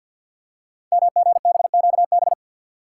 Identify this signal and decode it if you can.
Morse code, heard as a keyed tone, about 46 wpm, 700 Hz. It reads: MGBQB
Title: Code Groups - 5 Letters - 50wpm